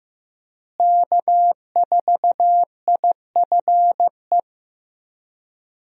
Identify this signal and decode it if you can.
Morse code, heard as a keyed tone, about 15 wpm, 700 Hz. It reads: K4IFE
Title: USA Call Signs - 5 Characters - Repeated 2x - 15wpm